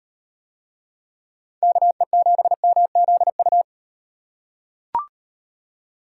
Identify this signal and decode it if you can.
Morse code, heard as a keyed tone, about 37 wpm, 700 Hz. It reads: KE7MZU